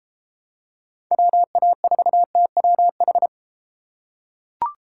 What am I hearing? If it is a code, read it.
Morse code, heard as a keyed tone, about 32 wpm, 700 Hz. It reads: WA4TWH